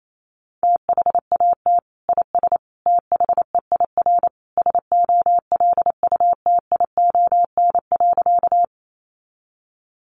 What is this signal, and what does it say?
Morse code, heard as a keyed tone, about 28 wpm, 700 Hz. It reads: THAT IS THEIR SOLUTION.
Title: Sentences from Top 1300 Words 28wpm